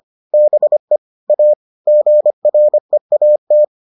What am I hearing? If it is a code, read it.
Morse code, heard as a keyed tone, about 25 wpm, 600 Hz. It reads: BE A GREAT